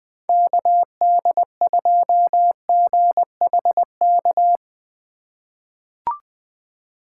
Morse code, heard as a keyed tone, about 20 wpm, 700 Hz. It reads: KD2GHK
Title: USA Call Signs - 6 Characters - 3x Character Spacing - 20wpm